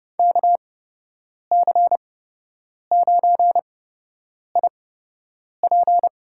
Morse code, heard as a keyed tone, about 30 wpm, spaced as for 8 wpm, 700 Hz. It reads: KC9IP
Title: USA Call Signs - 6 Characters - 8x Character Spacing - 30wpm